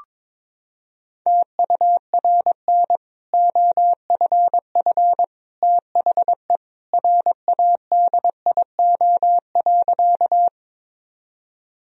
Morse code, heard as a keyed tone, about 22 wpm, 700 Hz. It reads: TURN OFF THE RADIO.
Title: Sentences from Top 1000 Words - Review Part 3 - 22wpm